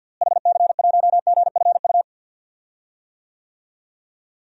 Morse code, heard as a keyed tone, about 51 wpm, 700 Hz. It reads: HC1GFU
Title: DXCC-100 Call Signs - 6 Characters - Repeated 3x - 50wpm